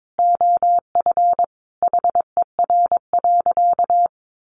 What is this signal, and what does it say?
Morse code, heard as a keyed tone, about 22 wpm, 700 Hz. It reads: OF HER.
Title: Sentences from Top 1100 Words 22wpm